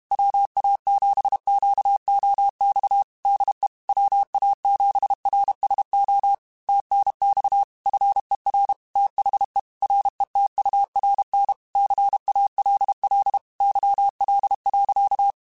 Morse code, heard as a keyed tone, about 32 wpm, 800 Hz. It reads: WA7QOX DE WA7RSO TNX FER THE RETURN CALL YL.
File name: QSOs / QSOs-mp3s / WA7QOX 2@30